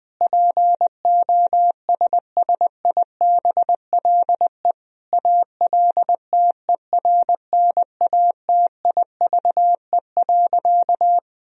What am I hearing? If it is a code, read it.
Morse code, heard as a keyed tone, about 20 wpm, 700 Hz. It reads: POSSIBLE ALTERNATIVE.
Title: Sentences from Top 1000 Words - Review Part 4 - 20wpm